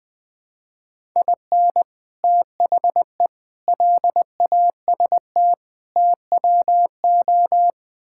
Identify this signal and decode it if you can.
Morse code, heard as a keyed tone, about 20 wpm, 700 Hz. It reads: IN THE LAST TWO
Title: Build A Phrase - 5-Word Phrases - 20wpm